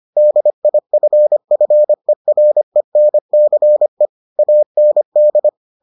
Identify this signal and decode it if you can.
Morse code, heard as a keyed tone, about 25 wpm, 600 Hz. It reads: DIFFERENCE AND